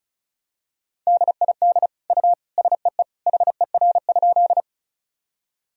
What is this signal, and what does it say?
Morse code, heard as a keyed tone, about 34 wpm, 700 Hz. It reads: DID U SEE HER?
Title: Sentences from Top 100 Words and Abbreviations 35wpm